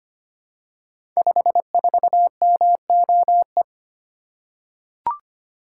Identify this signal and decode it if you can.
Morse code, heard as a keyed tone, about 25 wpm, 700 Hz. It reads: 54MOE
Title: Code Groups - 5 Letter:Numbers - 25wpm